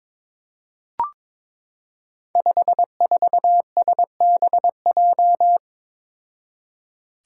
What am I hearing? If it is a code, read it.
Morse code, heard as a keyed tone, about 22 wpm, 700 Hz. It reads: 54SBJ
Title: Five Challenging Letter-Number Combinations 22wpm